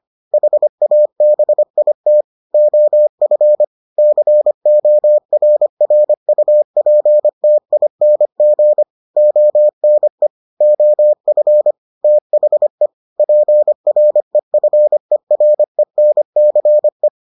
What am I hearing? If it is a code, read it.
Morse code, heard as a keyed tone, about 25 wpm, 600 Hz. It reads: HABIT OF CORRUPTING ONE OF THE PREFERENCE